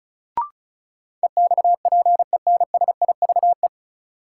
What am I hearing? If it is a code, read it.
Morse code, heard as a keyed tone, about 36 wpm, 700 Hz. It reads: EXPENSIVE
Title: Sentences from Top 1500 Words - New Word Drill Alt - 35wpm